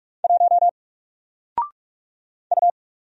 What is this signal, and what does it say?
Morse code, heard as a keyed tone, about 44 wpm, 700 Hz. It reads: 1 U
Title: Single Character (Letter:Number:Punctuation:Prosign) 45wpm